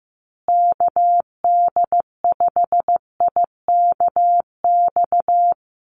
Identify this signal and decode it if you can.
Morse code, heard as a keyed tone, about 15 wpm, 700 Hz. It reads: KD5IKX
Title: USA Call Signs Repeated 2x 15wpm